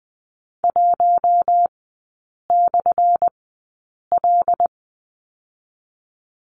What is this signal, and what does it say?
Morse code, heard as a keyed tone, about 20 wpm, 700 Hz. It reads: 1 / L
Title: 178 Lesson 29e - Slash - Three Characters 20wpm